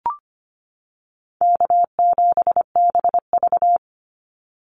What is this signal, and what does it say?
Morse code, heard as a keyed tone, about 25 wpm, 700 Hz. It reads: K7BV
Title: Call Signs - 25wpm